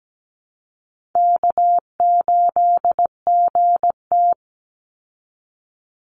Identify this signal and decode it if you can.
Morse code, heard as a keyed tone, about 17 wpm, 700 Hz. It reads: K8GT
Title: Easy Call Signs Repeated 2x - 17wpm